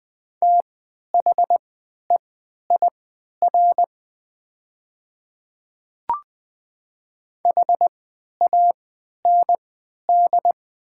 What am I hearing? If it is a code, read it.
Morse code, heard as a keyed tone, about 20 wpm, spaced as for 11 wpm, 690 Hz. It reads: THEIR HAND
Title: 3x Character Spacing - Top 300 Words - 20wpm